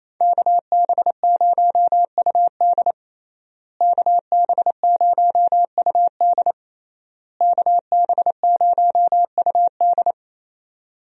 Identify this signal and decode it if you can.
Morse code, heard as a keyed tone, about 28 wpm, 700 Hz. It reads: KB0UD KB0UD KB0UD
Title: USA Call Signs - 5 Characters - Repeated 3x - 28wpm